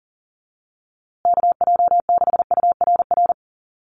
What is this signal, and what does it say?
Morse code, heard as a keyed tone, about 40 wpm, 700 Hz. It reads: KJ6URR
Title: USA Call Signs Repeated 2x 40wpm